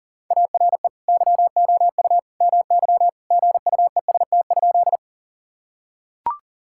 Morse code, heard as a keyed tone, about 40 wpm, 700 Hz. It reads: ARE YOU MY GUEST?
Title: Sentences from Top 1600 Words 40wpm